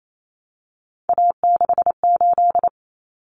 Morse code, heard as a keyed tone, about 28 wpm, 700 Hz. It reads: A68
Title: Common Cut Numbers - 3-Digit 28wpm